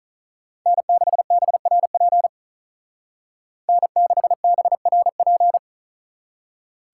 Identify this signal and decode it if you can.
Morse code, heard as a keyed tone, about 46 wpm, 700 Hz. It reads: N6BRP N6BRP